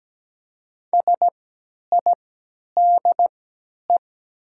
Morse code, heard as a keyed tone, about 17 wpm, spaced as for 10 wpm, 700 Hz. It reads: SIDE
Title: 3x Character Spacing - Top 200 Words - 17wpm